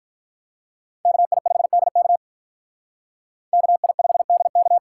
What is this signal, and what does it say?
Morse code, heard as a keyed tone, about 43 wpm, 700 Hz. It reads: KI5DK KI5DK
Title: Speed Racing - USA Call Signs 35wpm